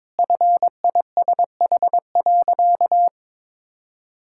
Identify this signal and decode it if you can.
Morse code, heard as a keyed tone, about 22 wpm, 700 Hz. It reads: FISH.
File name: practice / 22wpm / Sentences with 8-Word Phrases - 22wpm